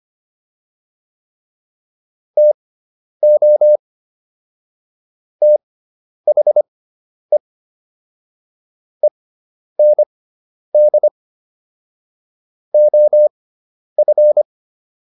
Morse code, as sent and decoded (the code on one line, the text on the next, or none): - --- / - .... / . -. -.. / --- ..-.
TO THE END OF